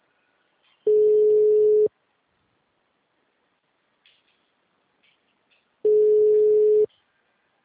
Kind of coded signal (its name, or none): telephone call progress tone